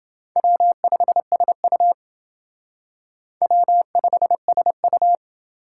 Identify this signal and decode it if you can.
Morse code, heard as a keyed tone, about 28 wpm, 700 Hz. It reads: W5SU W5SU